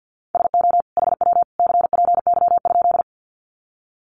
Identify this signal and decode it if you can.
Morse code, heard as a keyed tone, about 50 wpm, 700 Hz. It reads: SO HW CPY?
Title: Common QSO Phrases 50wpm